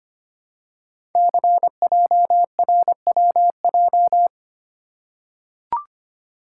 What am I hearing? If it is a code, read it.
Morse code, heard as a keyed tone, about 25 wpm, 700 Hz. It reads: CJRWJ
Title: Code Groups - 5 Letters - 25wpm